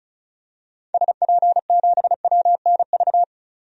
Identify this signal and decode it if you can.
Morse code, heard as a keyed tone, about 34 wpm, 700 Hz. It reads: SP7WNV
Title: DXCC-100 Call Signs - 6 Characters - 35wpm